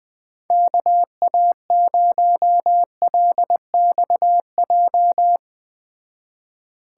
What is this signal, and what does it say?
Morse code, heard as a keyed tone, about 21 wpm, 700 Hz. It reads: KA0LXJ